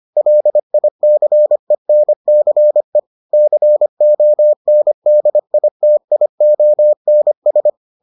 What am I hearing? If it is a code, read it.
Morse code, heard as a keyed tone, about 25 wpm, 600 Hz. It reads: LICENCE CONDITIONS